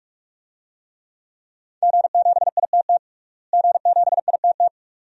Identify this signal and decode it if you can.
Morse code, heard as a keyed tone, about 46 wpm, 700 Hz. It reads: G7ITT G7ITT
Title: DXCC-100 Call Signs - 5 Characters - Repeated 2x - 45wpm